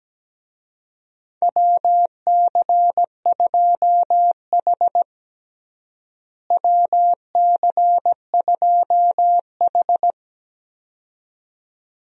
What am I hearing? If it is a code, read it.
Morse code, heard as a keyed tone, about 17 wpm, 700 Hz. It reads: WC2H WC2H
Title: USA Call Signs - 4 Characters - Repeated 2x - 17wpm